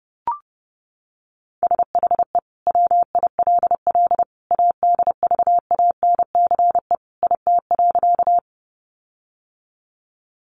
Morse code, heard as a keyed tone, about 30 wpm, 700 Hz. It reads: SHE WILL ADVANCE IT.
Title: Sentences from Top 1400 Words 30wpm